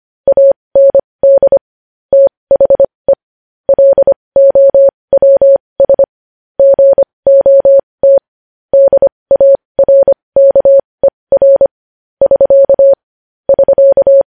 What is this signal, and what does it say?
Morse code, heard as a keyed tone, about 25 wpm, spaced as for 20 wpm, 550 Hz. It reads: AND THE LOWS GOT DARKER <SK> <SK>
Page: News Headlines in Morse Code at 20 WPM